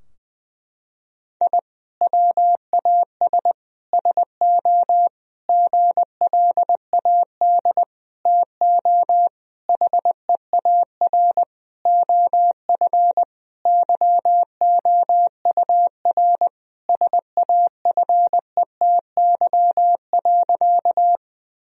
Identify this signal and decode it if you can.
Morse code, heard as a keyed tone, about 21 wpm, 700 Hz. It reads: I WAS SO GLAD TO HEAR OF YOUR SAFETY.
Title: Sentences from Top 1000 Words - Review Part 4 - 20wpm